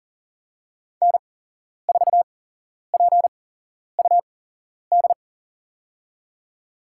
Morse code, heard as a keyed tone, about 40 wpm, spaced as for 11 wpm, 690 Hz. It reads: N4PUD
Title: USA Call Signs - 8x Character Spacing - 40wpm